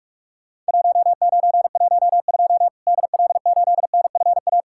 Morse code, heard as a keyed tone, about 46 wpm, 700 Hz. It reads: 1912 DL8NFA
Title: Worked All Europe DX Contest - QTC Exchange - 45wpm